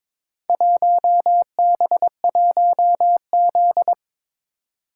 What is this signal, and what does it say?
Morse code, heard as a keyed tone, about 22 wpm, 700 Hz. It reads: 1B1Z
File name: practice / 22wpm / Four Letter-Number Combinations - 22wpm